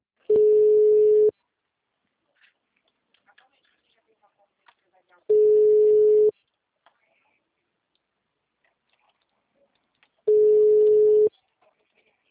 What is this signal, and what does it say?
Ringback tone (Europe, 425 Hz), with speech.